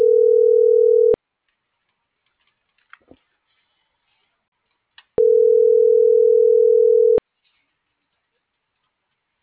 Ringback tone (North America), 440 + 480 Hz.